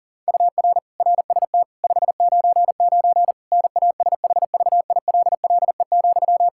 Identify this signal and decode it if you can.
Morse code, heard as a keyed tone, about 40 wpm, 700 Hz. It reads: UR RST 599 NASHVILLE,
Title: Practice QSO - Exchange 1 - Beginning - 40wpm